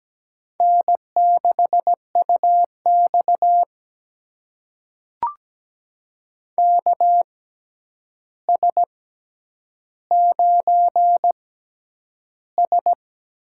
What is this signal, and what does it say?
Morse code, heard as a keyed tone, about 17 wpm, spaced as for 9 wpm, 700 Hz. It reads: N6UX KS9S
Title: USA Call Signs - 4 Characters - 6x Character Spacing - 17wpm